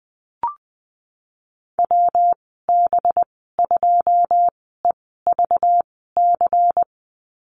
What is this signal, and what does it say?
Morse code, heard as a keyed tone, about 20 wpm, spaced as for 15 wpm, 700 Hz. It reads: WB2EVC
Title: 193 Lesson 31g - Letter v - Call Signs 20wpm